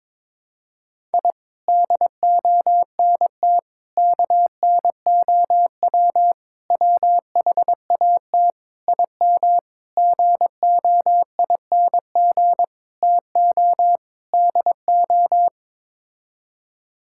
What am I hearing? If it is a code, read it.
Morse code, heard as a keyed tone, about 22 wpm, 700 Hz. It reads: I DONT KNOW WHAT IM GOING TO DO
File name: practice / 22wpm / Sets of 8 Words - 22wpm